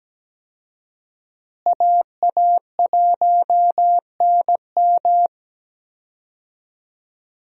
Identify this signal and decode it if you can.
Morse code, heard as a keyed tone, about 17 wpm, 700 Hz. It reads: AA1NM
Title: USA Call Signs - 5 Characters - Repeated 3x - 17wpm